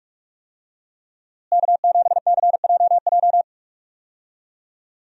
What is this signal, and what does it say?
Morse code, heard as a keyed tone, about 41 wpm, 700 Hz. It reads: K7CJJ
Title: USA Call Signs - 5 Characters - Repeated 3x - 45wpm